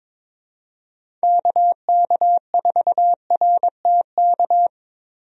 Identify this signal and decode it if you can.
Morse code, heard as a keyed tone, about 22 wpm, 700 Hz. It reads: KK4RTK
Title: USA Call Signs - 6 Characters - Repeated 2x - 22wpm